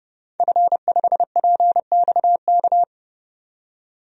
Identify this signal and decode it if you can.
Morse code, heard as a keyed tone, about 31 wpm, 700 Hz. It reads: F5PXK
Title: DXCC-100 Call Signs - 5 Characters - 30wpm